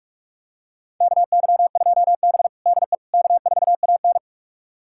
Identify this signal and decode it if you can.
Morse code, heard as a keyed tone, about 44 wpm, 690 Hz. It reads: KY2B DE K4AN
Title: USA Callsign DE Callsign - 4 Characters - 45wpm